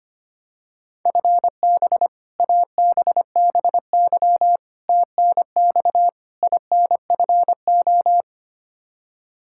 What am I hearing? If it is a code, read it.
Morse code, heard as a keyed tone, about 25 wpm, 700 Hz. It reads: FB ABBY TNX INFO
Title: Practice QSO - Exchange 3 - More Info - 25wpm